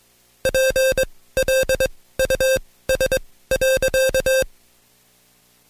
Morse code, heard as a keyed tone, about 21 wpm, spaced as for 16 wpm, 530 Hz. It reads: PLUS.
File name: Code Quick / AudioDownld2 / Seg134-Stepper7 15WPM